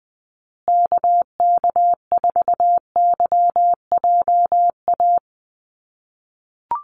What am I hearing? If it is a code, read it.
Morse code, heard as a keyed tone, about 20 wpm, 700 Hz. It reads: KK4YJA
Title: USA Call Signs Repeated 2x 20wpm